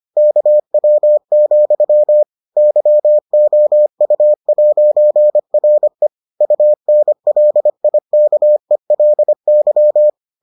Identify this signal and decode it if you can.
Morse code, heard as a keyed tone, about 25 wpm, 600 Hz. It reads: KW, YOU'RE UNLIKELY